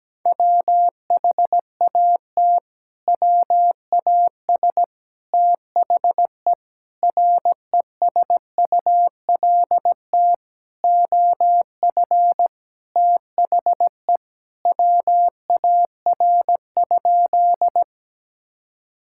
Morse code, heard as a keyed tone, about 17 wpm, 700 Hz. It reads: WHAT WAS THE RESULT OF THE WAR?
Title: Sentences from Top 1000 Words - Review Part 5 - 17wpm